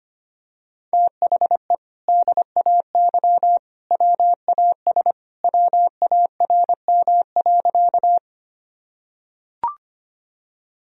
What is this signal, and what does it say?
Morse code, heard as a keyed tone, about 25 wpm, 700 Hz. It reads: THE DAY WAS WARM.
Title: Sentences from Top 1000 Words - Review Part 2 - 25wpm